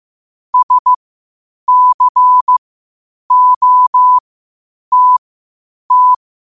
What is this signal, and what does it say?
Morse code, heard as a keyed tone, about 15 wpm, spaced as for 8 wpm, 1000 Hz. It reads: SCOTT
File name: qotdfiles / 8wpm